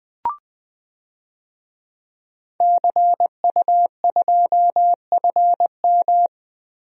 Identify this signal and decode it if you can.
Morse code, heard as a keyed tone, about 21 wpm, 700 Hz. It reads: CU2FM
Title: DXCC-100 Call Signs - 5 Characters - Repeated 3x - 20wpm